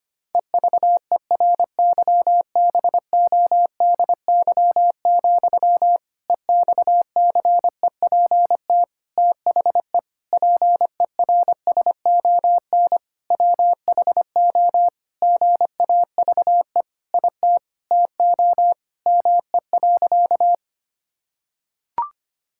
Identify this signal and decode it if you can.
Morse code, heard as a keyed tone, about 25 wpm, 700 Hz. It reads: EVERYBODY, EXCEPT THE PERSON WHO GAVE IT TO ME.
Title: Sentences from Top 1600 Words 25wpm